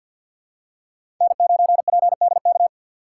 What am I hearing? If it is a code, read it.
Morse code, heard as a keyed tone, about 51 wpm, 700 Hz. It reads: N9PDK
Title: USA Call Signs - 5 Characters - 6x Character Spacing - 50wpm